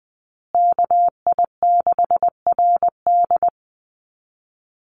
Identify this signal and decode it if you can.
Morse code, heard as a keyed tone, about 20 wpm, 700 Hz. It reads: KI6RD
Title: USA Call Signs Repeated 2x 20wpm